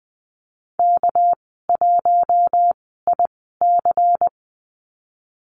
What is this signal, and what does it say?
Morse code, heard as a keyed tone, about 19 wpm, spaced as for 14 wpm, 700 Hz. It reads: K1IC
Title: 207 Lesson 33g - Letter j - Call Signs 20wpm